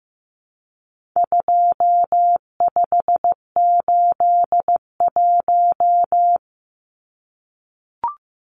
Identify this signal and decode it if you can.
Morse code, heard as a keyed tone, about 15 wpm, 700 Hz. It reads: 2581
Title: Numbers - 4-Digit 15wpm